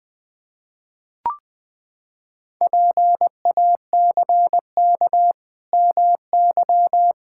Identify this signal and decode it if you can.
Morse code, heard as a keyed tone, about 20 wpm, 700 Hz. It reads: PACK MY
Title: Pangram Challenge v2 - 20wpm